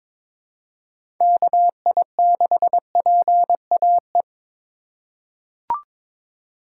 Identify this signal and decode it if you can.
Morse code, heard as a keyed tone, about 22 wpm, 700 Hz. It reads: KI6PAE
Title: USA Call Signs - 6 Characters - 2x Character Spacing - 22wpm